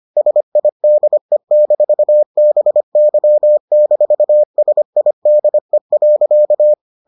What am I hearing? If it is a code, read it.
Morse code, heard as a keyed tone, about 25 wpm, 600 Hz. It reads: SIDE-BY-SIDE.